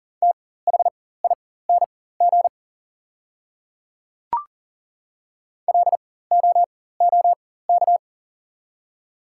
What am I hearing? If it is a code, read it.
Morse code, heard as a keyed tone, about 40 wpm, spaced as for 19 wpm, 700 Hz. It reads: THING LOOK
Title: 4x Character Spacing - Top 200 Words - 40wpm